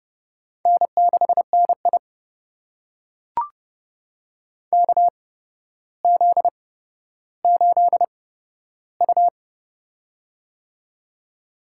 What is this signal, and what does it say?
Morse code, heard as a keyed tone, about 30 wpm, spaced as for 13 wpm, 700 Hz. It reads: N6NI KZ8U
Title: USA Call Signs - 4 Characters - 8x Character Spacing - 30wpm